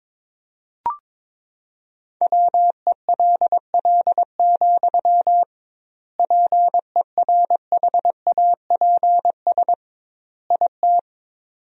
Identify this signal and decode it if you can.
Morse code, heard as a keyed tone, about 22 wpm, 700 Hz. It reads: WELL, PERHAPS IT
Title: 2x Word-Spacing - Sentences from Top 500 Words - 22wpm